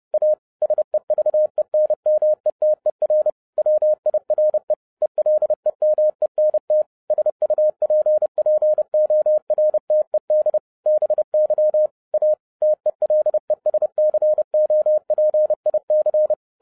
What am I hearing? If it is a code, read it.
Morse code, heard as a keyed tone, about 30 wpm, 610 Hz. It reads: A SEVENMETER WIRE ELEMENT SUPPORTED BY A TELESCOPIC